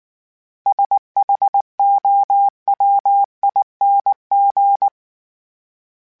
Morse code, heard as a keyed tone, about 19 wpm, 800 Hz. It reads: SHOWING